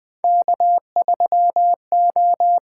Morse code, heard as a keyed tone, about 20 wpm, 700 Hz. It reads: K3O
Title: USA Call Signs - 8x Character Spacing - 20wpm